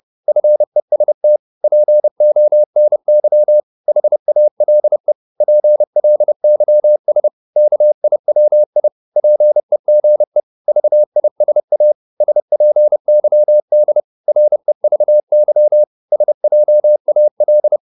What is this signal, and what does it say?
Morse code, heard as a keyed tone, about 30 wpm, 600 Hz. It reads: FEST PONY HALE PLYS KIWI PEGE VISA SPYD REVY SJAL
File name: Korte ord DK 30wpm